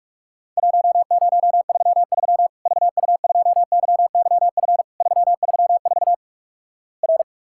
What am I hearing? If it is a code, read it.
Morse code, heard as a keyed tone, about 46 wpm, 700 Hz. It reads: 1033 VU2YYF 334 R
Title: Worked All Europe DX Contest - QTC Exchange - 45wpm